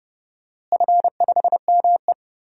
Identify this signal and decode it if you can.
Morse code, heard as a keyed tone, about 30 wpm, 700 Hz. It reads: F5ME
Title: DXCC-100 Call Signs - 4 Characters - Repeated 3x - 30wpm